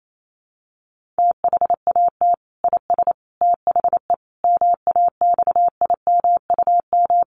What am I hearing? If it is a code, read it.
Morse code, heard as a keyed tone, about 28 wpm, 700 Hz. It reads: THAT IS THE MAXIMUM
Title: Sentences from Top 1300 Words 28wpm